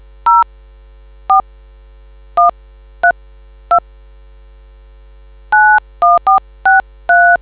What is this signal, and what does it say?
DTMF, heard as telephone touch tones, tones of about 160 ms. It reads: 04132 91463